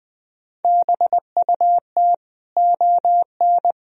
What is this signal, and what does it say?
Morse code, heard as a keyed tone, about 20 wpm, 700 Hz. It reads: BUT ON
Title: Sentences from Top 1000 Words - Review Part 1 - 20wpm